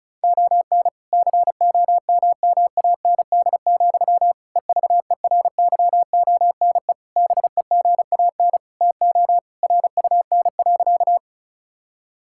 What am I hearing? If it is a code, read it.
Morse code, heard as a keyed tone, about 36 wpm, 690 Hz. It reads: ON COMMAND, EVERYONE BEGAN TO RUN.
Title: Sentences from Top 1600 Words 35wpm